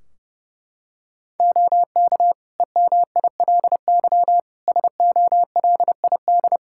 Morse code, heard as a keyed tone, about 30 wpm, 700 Hz. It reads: OK EMILY SOLID
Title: Practice QSO - Exchange 3 - More Info - 30wpm